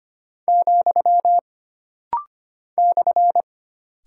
Morse code, heard as a keyed tone, about 25 wpm, 700 Hz. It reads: , /
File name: Single Punctuation-Prosign - Rapid-Fire - 25wpm